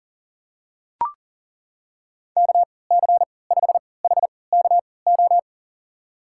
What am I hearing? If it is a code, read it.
Morse code, heard as a keyed tone, about 40 wpm, spaced as for 23 wpm, 700 Hz. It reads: KC5HKO
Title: USA Call Signs - 6 Characters - 3x Character Spacing - 40wpm